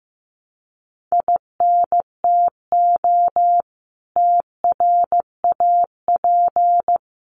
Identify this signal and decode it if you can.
Morse code, heard as a keyed tone, about 15 wpm, 700 Hz. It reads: INTO TRAP
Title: Common QSO Phrases 15wpm